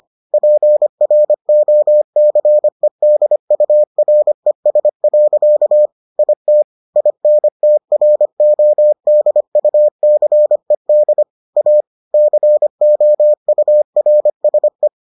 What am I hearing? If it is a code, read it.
Morse code, heard as a keyed tone, about 25 wpm, 600 Hz. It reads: PROCEDURES. IT INTRODUCED A COURSE